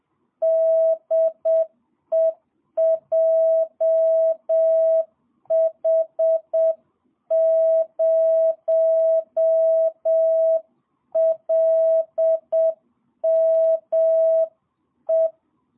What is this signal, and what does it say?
Morse code, heard as a keyed tone, about 7 wpm, 650 Hz. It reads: DE JH0LME
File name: jh8cbh / 2022cw